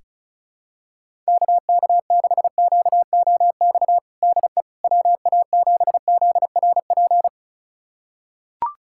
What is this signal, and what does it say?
Morse code, heard as a keyed tone, about 36 wpm, 700 Hz. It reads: KK6QOX DE WA7ZRP
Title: USA Callsign DE Callsign - 6 Characters - 35wpm